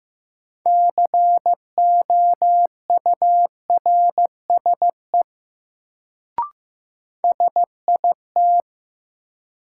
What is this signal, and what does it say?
Morse code, heard as a keyed tone, about 15 wpm, 700 Hz. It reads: COURSE SIT
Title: Sentences from Top 500 Words - New Word Drill - 15wpm